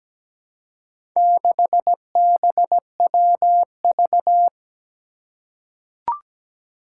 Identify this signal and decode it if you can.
Morse code, heard as a keyed tone, about 17 wpm, 700 Hz. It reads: 6BWV
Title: Four Challenging Letter-Number Combinations 17wpm